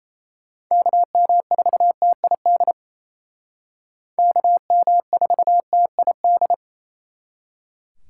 Morse code, heard as a keyed tone, about 36 wpm, 700 Hz. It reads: KM4TID KM4TID